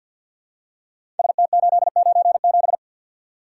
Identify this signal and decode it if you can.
Morse code, heard as a keyed tone, about 47 wpm, 700 Hz. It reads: ST897
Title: Code Groups - 5 Letter:Numbers - 50wpm